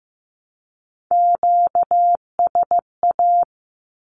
Morse code, heard as a keyed tone, about 15 wpm, 700 Hz. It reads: QSA
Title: Q-Codes 15wpm